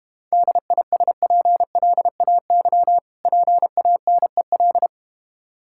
Morse code, heard as a keyed tone, about 32 wpm, 700 Hz. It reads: DISPLAY PANEL